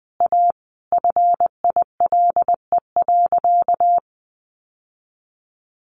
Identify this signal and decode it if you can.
Morse code, heard as a keyed tone, about 20 wpm, 700 Hz. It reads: A FILE.
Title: Sentences from Top 1200 Words 20wpm